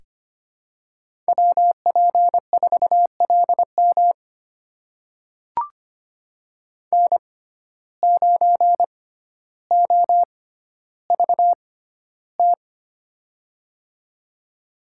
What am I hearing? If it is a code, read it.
Morse code, heard as a keyed tone, about 25 wpm, spaced as for 13 wpm, 700 Hz. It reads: WP4LM N9OVT
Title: USA Call Signs - 5 Characters - 6x Character Spacing - 25wpm